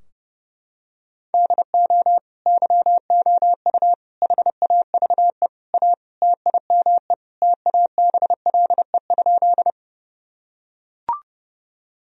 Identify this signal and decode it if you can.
Morse code, heard as a keyed tone, about 32 wpm, 700 Hz. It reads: DO YOU HAVE A TIME TABLE?
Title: Sentences from Top 1000 Words - Review Part 2 - 30wpm